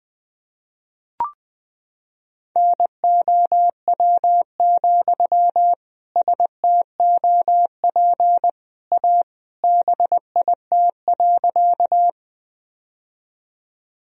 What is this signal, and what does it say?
Morse code, heard as a keyed tone, about 20 wpm, 700 Hz. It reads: NOW, STOP A BIT.
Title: Sentences from Top 1000 Words - Review Part 1 - 20wpm